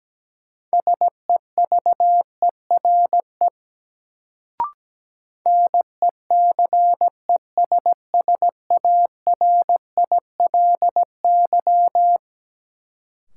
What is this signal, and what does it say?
Morse code, heard as a keyed tone, about 17 wpm, 700 Hz. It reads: SEVERE NECESSARILY
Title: Sentences from Top 1400 Words - New Word Drill - 17wpm